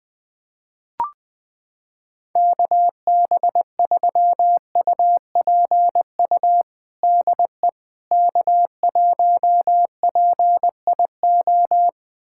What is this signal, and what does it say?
Morse code, heard as a keyed tone, about 21 wpm, 700 Hz. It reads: KB3UPU DE K1PIO
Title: USA Callsign DE Callsign 20wpm